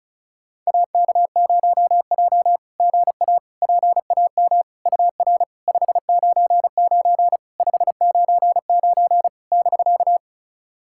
Morse code, heard as a keyed tone, about 36 wpm, 690 Hz. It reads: AK0J GA PAM UR 599 599 <BK>
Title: SOTA QSO Practice 35wpm